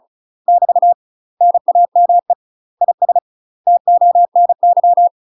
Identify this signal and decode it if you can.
Morse code, heard as a keyed tone, about 34 wpm, 700 Hz. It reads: <BT> NAME IS TONY
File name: UK MASTER QSOs TRAINING/35WPM